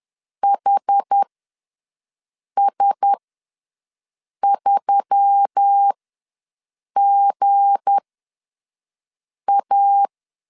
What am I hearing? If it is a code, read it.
Morse code, heard as a keyed tone, about 10 wpm, spaced as for 5 wpm, 800 Hz. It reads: HS3GA
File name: LEZIONI / 7 GF9Z / 7 G ws 10wpm 3843